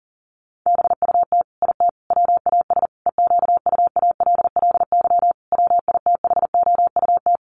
Morse code, heard as a keyed tone, about 40 wpm, 700 Hz. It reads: BUT IT WAS EQUALLY WITHOUT
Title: Sentences from Top 1400 Words 40wpm